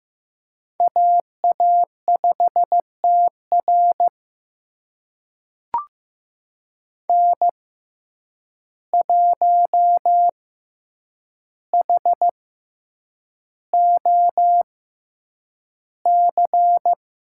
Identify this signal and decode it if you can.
Morse code, heard as a keyed tone, about 15 wpm, spaced as for 8 wpm, 700 Hz. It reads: AA5TR N1HOC
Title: USA Call Signs - 5 Characters - 6x Character Spacing - 15wpm